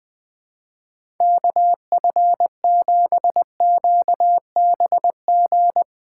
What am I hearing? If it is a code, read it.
Morse code, heard as a keyed tone, about 20 wpm, 700 Hz. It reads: KF7QBG
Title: USA Call Signs - 6 Characters - 4x Character Spacing - 20wpm